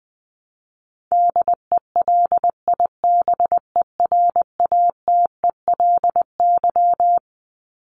Morse code, heard as a keyed tone, about 20 wpm, 700 Hz. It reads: DELIBERATELY
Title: Top 1751-2000 Words 20wpm